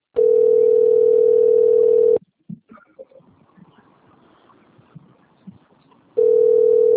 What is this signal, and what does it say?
Ringback tone (North America), 440 + 480 Hz.